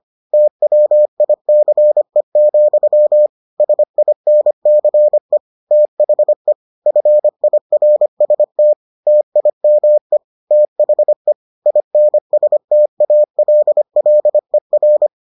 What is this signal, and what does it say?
Morse code, heard as a keyed tone, about 25 wpm, 600 Hz. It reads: TWICE, SINCE THE FIRST TIME THE INSTALLER